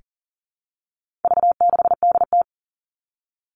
Morse code, heard as a keyed tone, about 40 wpm, 700 Hz. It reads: V6DT
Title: All Cut Numbers - 4-Digit 40wpm